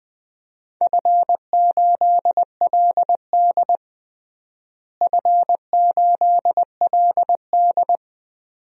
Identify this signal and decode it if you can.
Morse code, heard as a keyed tone, about 20 wpm, 700 Hz. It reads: F8LD F8LD
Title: DXCC-100 Call Signs - 4 Characters - Repeated 2x - 20wpm